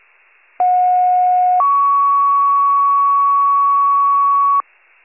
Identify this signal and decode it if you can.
Two-tone page, heard as a one-second tone, then a three-second tone, about 730 then 1120 Hz. It is heard from 0.6 to 4.6 s.